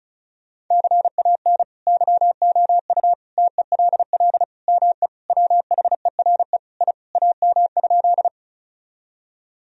Morse code, heard as a keyed tone, about 34 wpm, 700 Hz. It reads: CAN YOU TELL ME WHERE I AM?
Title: Sentences from Top 1000 Words - Review Part 2 - 35wpm